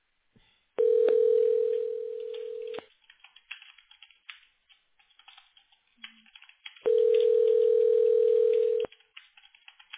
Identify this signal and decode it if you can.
Ringback tone (North America), 440 + 480 Hz.